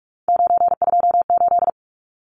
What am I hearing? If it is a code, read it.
Morse code, heard as a keyed tone, about 44 wpm, 700 Hz. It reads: 928
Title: Numbers - 3-Digit 45wpm